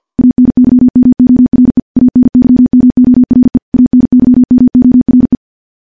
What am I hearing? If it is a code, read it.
Morse code, heard as a keyed tone, about 50 wpm, 250 Hz. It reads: ANYMORE ANYMORE ANYMORE
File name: morse / morse-training-50WPM